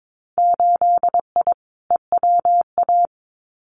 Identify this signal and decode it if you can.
Morse code, heard as a keyed tone, about 22 wpm, 700 Hz. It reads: 8I EWA
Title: Winter Field Day Exchanges Encore 22wpm